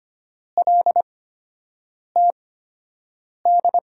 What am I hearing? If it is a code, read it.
Morse code, heard as a keyed tone, about 25 wpm, spaced as for 7 wpm, 700 Hz. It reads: LTD
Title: USA Call Signs - 5 Characters - 8x Character Spacing - 25wpm